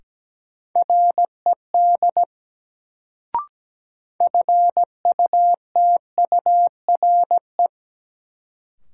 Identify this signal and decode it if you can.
Morse code, heard as a keyed tone, about 17 wpm, 700 Hz. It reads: RED FUTURE
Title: Sentences from Top 700 Words - New Word Drill - 17wpm